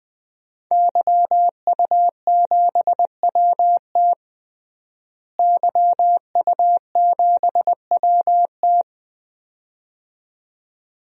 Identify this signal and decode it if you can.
Morse code, heard as a keyed tone, about 20 wpm, 700 Hz. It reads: YU7WT YU7WT
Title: DXCC-100 Call Signs - 5 Characters - Repeated 2x - 20wpm